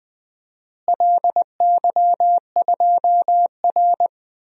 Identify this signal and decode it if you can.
Morse code, heard as a keyed tone, about 20 wpm, 700 Hz. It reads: LY2R
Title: DXCC-100 Call Signs - 4 Characters - Repeated 2x - 20wpm